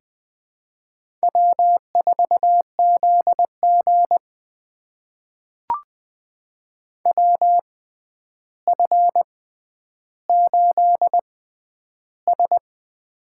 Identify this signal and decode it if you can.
Morse code, heard as a keyed tone, about 20 wpm, spaced as for 10 wpm, 700 Hz. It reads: W4ZG WF8S
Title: USA Call Signs - 6x Character Spacing - 20wpm